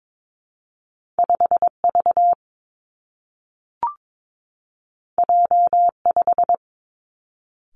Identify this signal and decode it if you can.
Morse code, heard as a keyed tone, about 22 wpm, 700 Hz. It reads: 5 V J 5
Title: Two Challenging Letter-Number Combinations 22wpm